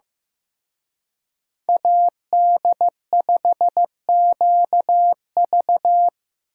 Morse code, heard as a keyed tone, about 18 wpm, 700 Hz. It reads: AD5QV
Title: Speed Racing - USA 2x2 Call Signs 15wpm